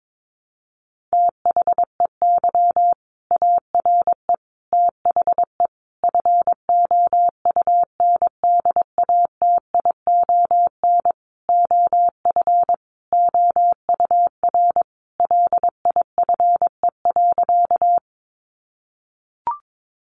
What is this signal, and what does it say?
Morse code, heard as a keyed tone, about 22 wpm, 700 Hz. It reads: THEY ARE THE FOUNDATION OF OUR LIFE.
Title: Sentences from Top 1200 Words 22wpm